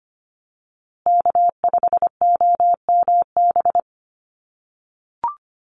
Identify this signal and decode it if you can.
Morse code, heard as a keyed tone, about 25 wpm, 700 Hz. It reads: K5OMB
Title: USA Call Signs Repeated 2x 25wpm